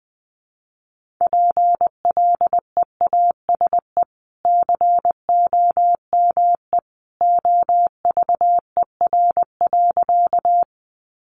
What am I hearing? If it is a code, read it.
Morse code, heard as a keyed tone, about 20 wpm, 700 Hz. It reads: PLEASE COME OVER.
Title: Sentences from Top 1500 Words 20wpm